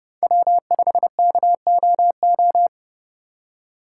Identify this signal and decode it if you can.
Morse code, heard as a keyed tone, about 30 wpm, 700 Hz. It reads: W5KOO
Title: USA Call Signs - 5 Characters - 3x Character Spacing - 30wpm